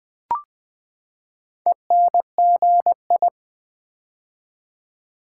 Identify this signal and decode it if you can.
Morse code, heard as a keyed tone, about 20 wpm, 700 Hz. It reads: ENGI
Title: Build A Word - 4 Letters At A Time - 11-Letter Words - 20wpm